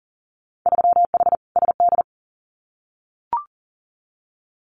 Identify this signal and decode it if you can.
Morse code, heard as a keyed tone, about 40 wpm, 700 Hz. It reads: 3H SD
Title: Winter Field Day Exchanges Encore 40wpm